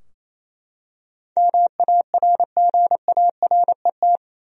Morse code, heard as a keyed tone, about 28 wpm, 700 Hz. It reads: MARGARET
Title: USA Common Names - 28wpm